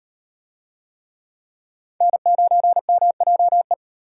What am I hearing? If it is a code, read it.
Morse code, heard as a keyed tone, about 37 wpm, 700 Hz. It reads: N9MJE